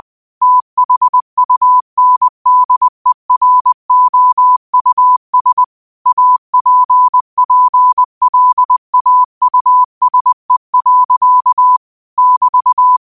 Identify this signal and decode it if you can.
Morse code, heard as a keyed tone, about 20 wpm, 1000 Hz. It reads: THUNDEROUS APPLAUSE. <BT>
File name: qotdfiles / 20wpm